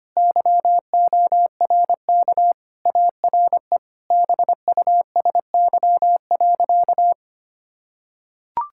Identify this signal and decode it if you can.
Morse code, heard as a keyed tone, about 25 wpm, 700 Hz. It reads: YORK ARE BUSY.
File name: practice / 25wpm / Sentences with 3-Word Phrases - 25wpm